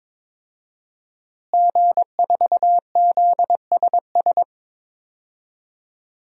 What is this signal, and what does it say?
Morse code, heard as a keyed tone, about 22 wpm, 700 Hz. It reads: G4ZSS
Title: DXCC-100 Call Signs - 5 Characters - Repeated 2x - 22wpm